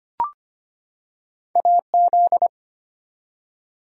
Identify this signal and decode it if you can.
Morse code, heard as a keyed tone, about 25 wpm, 700 Hz. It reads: AZ
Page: morse-code-ninja-practice-25wpm